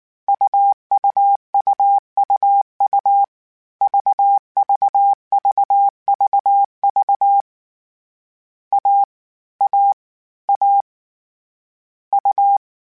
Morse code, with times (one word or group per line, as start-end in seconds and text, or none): 0.3-3.2 UUUUU
3.8-7.4 VVVVV
8.7-9.0 A
9.6-9.9 A
10.5-10.8 A
12.1-12.6 U